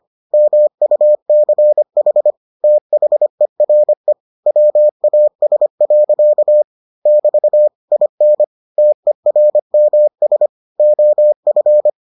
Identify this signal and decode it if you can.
Morse code, heard as a keyed tone, about 25 wpm, 600 Hz. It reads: MUCH THERE WAS. <BT> IN TERMS OF